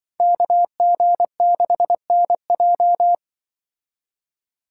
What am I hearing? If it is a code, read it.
Morse code, heard as a keyed tone, about 25 wpm, 700 Hz. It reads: KG6NJ
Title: Speed Racing - USA 2x2 Call Signs 20wpm